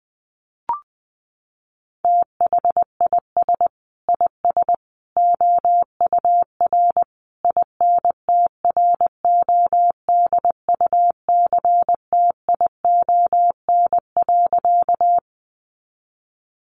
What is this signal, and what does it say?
Morse code, heard as a keyed tone, about 20 wpm, 700 Hz. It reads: THIS IS OUR INTRODUCTION.
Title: Sentences from Top 1000 Words 20wpm